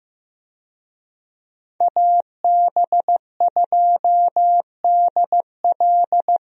Morse code, heard as a keyed tone, about 16 wpm, 700 Hz. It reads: AB2DL
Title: USA Call Signs - 5 Characters - Repeated 2x - 15wpm